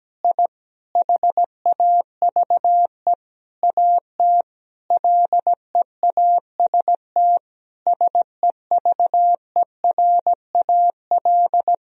Morse code, heard as a keyed tone, about 17 wpm, 700 Hz. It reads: I HAVE AT LEAST SEVERAL
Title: Sentences from Top 1000 Words - Review Part 2 - 17wpm